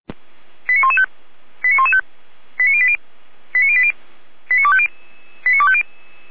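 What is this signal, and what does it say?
Five-tone selective call (ZVEI), heard as quick runs of five tones, tones of about 70 ms. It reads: 89115 89115 89080 89080 89250 89250
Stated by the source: ZVEI five-tone call received by SDR.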